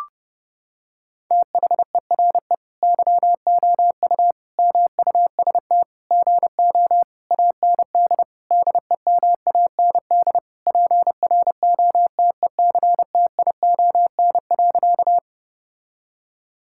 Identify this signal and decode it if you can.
Morse code, heard as a keyed tone, about 32 wpm, 700 Hz. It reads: THERE YOU MUST GO AND DEMAND PROTECTION.
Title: Sentences from Top 1600 Words 30wpm